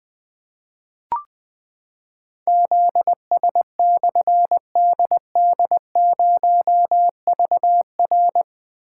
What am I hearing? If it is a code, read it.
Morse code, heard as a keyed tone, about 20 wpm, 700 Hz. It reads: ZS/DD0VR
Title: Difficult Call Signs 20wpm